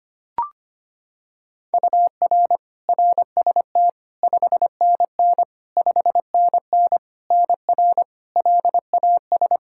Morse code, heard as a keyed tone, about 25 wpm, 700 Hz. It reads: UR RST 5NN 5NN NR LAS
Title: K1USN SST QSO Practice 25wpm